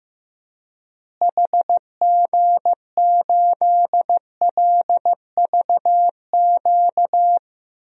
Morse code, heard as a keyed tone, about 15 wpm, 700 Hz. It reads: HG8LVQ
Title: DXCC-100 Call Signs - 6 Characters - Repeated 3x - 15wpm